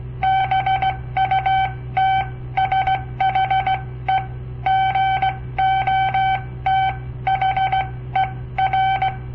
Morse code, heard as a keyed tone, about 17 wpm, spaced as for 14 wpm, 760 Hz. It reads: BUT SHE GOT HER
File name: papertapes / local / Code Practice-01